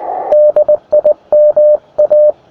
Morse code, heard as a keyed tone, about 19 wpm, 600 Hz. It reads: DIMA
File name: attachments / ua6ypt 15m